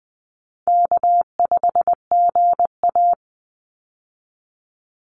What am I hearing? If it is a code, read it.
Morse code, heard as a keyed tone, about 20 wpm, 700 Hz. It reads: K5GA
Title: Call Signs - 20wpm